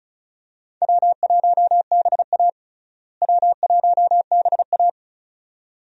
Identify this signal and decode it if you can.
Morse code, heard as a keyed tone, about 36 wpm, 700 Hz. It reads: W1BA W1BA
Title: USA Call Signs - 4 Characters - Repeated 2x - 35wpm